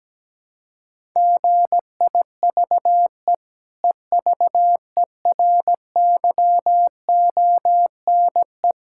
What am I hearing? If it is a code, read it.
Morse code, heard as a keyed tone, about 17 wpm, 700 Hz. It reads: GIVE EVERYONE
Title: Sentences from Top 1000 Words - Review Part 6 - 17wpm